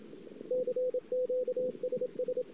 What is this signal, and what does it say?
Morse code, heard as a keyed tone, about 29 wpm, 500 Hz, with radio noise. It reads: CQSS
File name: ss-cw